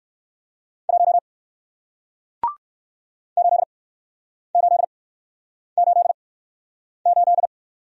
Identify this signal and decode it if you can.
Morse code, heard as a keyed tone, about 61 wpm, spaced as for 9 wpm, 700 Hz. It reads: 1 8888